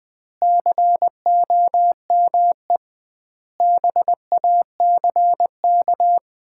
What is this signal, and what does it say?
Morse code, heard as a keyed tone, about 20 wpm, 700 Hz. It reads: COME BACK
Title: 2x Word-Spacing - Sets of 5 Words - 20wpm